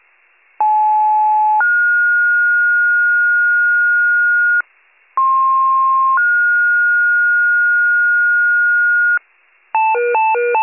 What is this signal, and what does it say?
Two-tone page, heard as a one-second tone, then a three-second tone, about 860 then 1430 Hz, about 1060 then 1430 Hz. It is heard from 0.6 to 9.2 s.